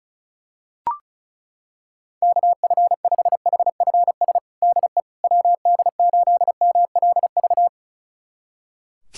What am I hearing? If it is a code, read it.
Morse code, heard as a keyed tone, about 36 wpm, 700 Hz. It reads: KF5HFS DE WD8MLV
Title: USA Callsign DE Callsign - 6 Characters - 35wpm